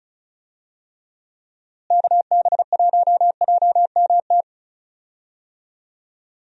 Morse code, heard as a keyed tone, about 34 wpm, 700 Hz. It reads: KB1JMT
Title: USA Call Signs - 6 Characters - Repeated 3x - 35wpm